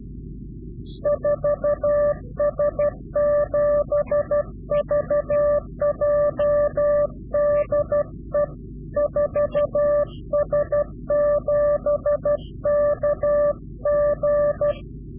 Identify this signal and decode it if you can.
Morse code, heard as a keyed tone, about 12 wpm, 590 Hz. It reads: 4S7VJ DE 4S7KG
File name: CW-QSO / Directional Call / Directional QSO